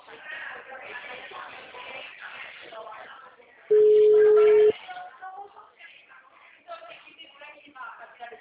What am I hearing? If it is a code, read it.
Ringback tone (Europe, 425 Hz), with speech.